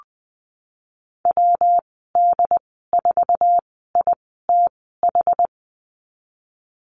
Morse code, heard as a keyed tone, about 20 wpm, spaced as for 15 wpm, 700 Hz. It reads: WD4ITH
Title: 122 Lesson 20g - Number 3 - Call Signs 20wpm